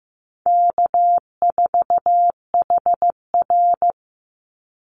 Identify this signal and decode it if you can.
Morse code, heard as a keyed tone, about 15 wpm, 700 Hz. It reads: K4HR
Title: Easy Call Signs Repeated 3x - 15wpm